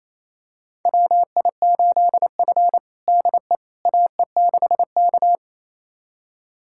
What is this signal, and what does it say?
Morse code, heard as a keyed tone, about 27 wpm, 700 Hz. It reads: WI8F DE AE6K
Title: USA Callsign DE Callsign - 4 Characters - 28wpm